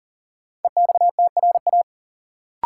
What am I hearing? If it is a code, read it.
Morse code, heard as a keyed tone, about 38 wpm, 700 Hz. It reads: EXTRA
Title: Sentences from Top 1000 Words - New Word Drill - 40wpm